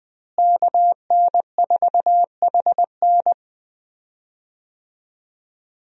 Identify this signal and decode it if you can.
Morse code, heard as a keyed tone, about 21 wpm, 700 Hz. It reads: KN4HN
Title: USA Call Signs - 5 Characters - Repeated 3x - 20wpm